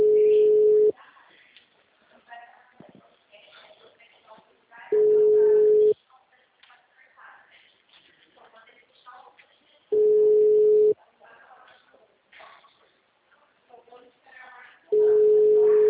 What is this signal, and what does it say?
Ringback tone (Europe, 425 Hz), with speech.
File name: out- -2005- - - .61488